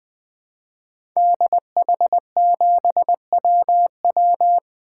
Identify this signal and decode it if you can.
Morse code, heard as a keyed tone, about 20 wpm, 700 Hz. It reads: DH7WW
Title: 2023 SOTA Activator Call Signs - 20wpm